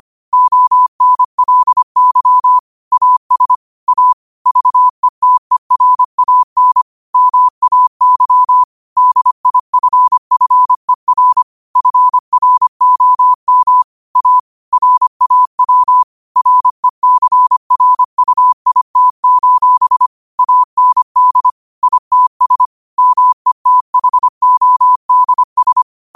Morse code, heard as a keyed tone, about 25 wpm, 1000 Hz. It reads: ONLY AS A VETERAN MAY DIFFER FROM A RAW RECRUIT: AND ITS METHODS
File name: qotdfiles / 25wpm